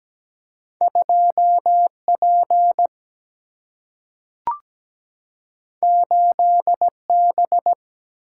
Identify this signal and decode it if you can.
Morse code, heard as a keyed tone, about 17 wpm, 700 Hz. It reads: 2 P 8 B
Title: Single-Letter Single-Number 17wpm